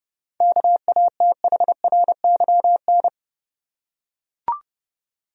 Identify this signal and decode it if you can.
Morse code, heard as a keyed tone, about 30 wpm, 700 Hz. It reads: KATHRYN
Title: USA Common Names - 30wpm